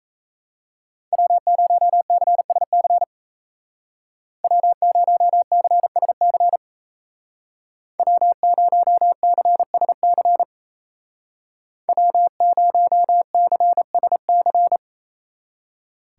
Morse code, heard as a keyed tone, about 37 wpm, 700 Hz. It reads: W0CSC W0CSC W0CSC W0CSC